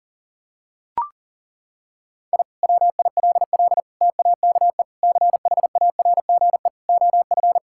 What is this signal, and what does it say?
Morse code, heard as a keyed tone, about 37 wpm, 700 Hz. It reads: I WILL TAKE CHARGE OF
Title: Sentences from Top 1000 Words - Review Part 6 - 40wpm